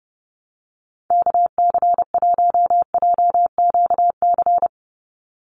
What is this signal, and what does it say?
Morse code, heard as a keyed tone, about 30 wpm, 700 Hz. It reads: KC1JQC
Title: USA Call Signs 30wpm